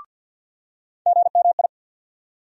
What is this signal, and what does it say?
Morse code, heard as a keyed tone, about 52 wpm, 700 Hz. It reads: GMI
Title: USA Call Sign Postfixes - 3 Letters 50wpm